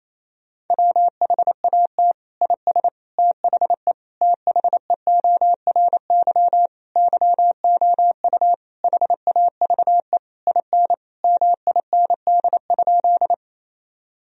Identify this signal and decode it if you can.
Morse code, heard as a keyed tone, about 28 wpm, 700 Hz. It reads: WHAT IS THE THEORY YOU HAVE IN MIND?
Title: Sentences from Top 1000 Words - Review Part 3 - 28wpm